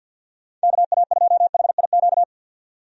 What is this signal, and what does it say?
Morse code, heard as a keyed tone, about 53 wpm, 700 Hz. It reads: KA2HIQ
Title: USA Call Signs - 6 Characters - 50wpm